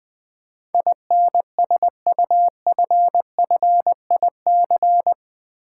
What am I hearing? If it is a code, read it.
Morse code, heard as a keyed tone, about 20 wpm, 700 Hz. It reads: INSUFFIC
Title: Build A Word - 4 Letters At A Time - 12-Letter Words - 20wpm